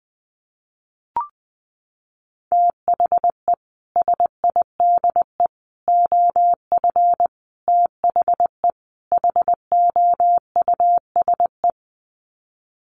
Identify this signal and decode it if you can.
Morse code, heard as a keyed tone, about 20 wpm, 700 Hz. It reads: THE SIDE OF THE HOUSE
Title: Sets of 5 Words Encore2 20wpm